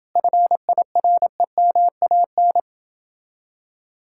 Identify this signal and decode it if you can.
Morse code, heard as a keyed tone, about 29 wpm, 700 Hz. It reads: FIREMAN